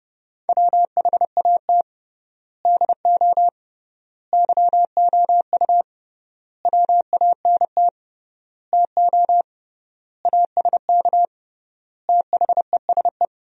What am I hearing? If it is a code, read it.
Morse code, heard as a keyed tone, about 30 wpm, 700 Hz. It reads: WHAT DO YOU WANT TO ASK THESE
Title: 3x Word-Spacing - Sentences from Top 200 Words - 30wpm